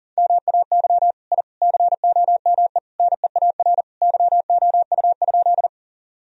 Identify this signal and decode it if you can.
Morse code, heard as a keyed tone, about 40 wpm, 700 Hz. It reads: MAY I COME NEAR YOU?
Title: Sentences from Top 1000 Words - Review Part 1 - 40wpm